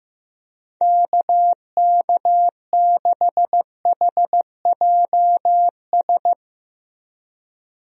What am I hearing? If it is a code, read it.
Morse code, heard as a keyed tone, about 15 wpm, 700 Hz. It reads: KK6HJS
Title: USA Call Signs - 6 Characters - Repeated 2x - 15wpm